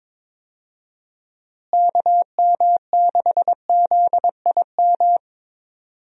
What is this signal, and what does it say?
Morse code, heard as a keyed tone, about 22 wpm, 700 Hz. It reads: KM6ZIM
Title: USA Call Signs - 6 Characters - Repeated 2x - 22wpm